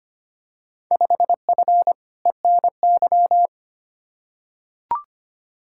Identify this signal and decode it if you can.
Morse code, heard as a keyed tone, about 25 wpm, 700 Hz. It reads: 5F ENY
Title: Field Day Exchanges Encore - 25wpm